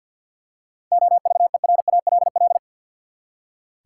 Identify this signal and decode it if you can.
Morse code, heard as a keyed tone, about 48 wpm, 700 Hz. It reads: OVERALL
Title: Seven Letter Words - 50wpm